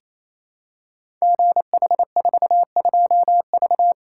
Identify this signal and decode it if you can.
Morse code, heard as a keyed tone, about 28 wpm, 700 Hz. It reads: GH42V
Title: Code Groups - 5 Letter:Numbers - 28wpm